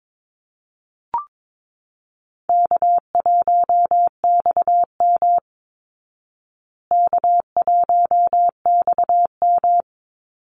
Morse code, heard as a keyed tone, about 22 wpm, 690 Hz. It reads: K1XM K1XM
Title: Easy Call Signs Repeated 2x - 22wpm